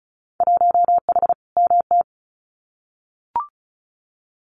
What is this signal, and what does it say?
Morse code, heard as a keyed tone, about 36 wpm, 700 Hz. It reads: 1H MT
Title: Winter Field Day Exchanges Encore 35wpm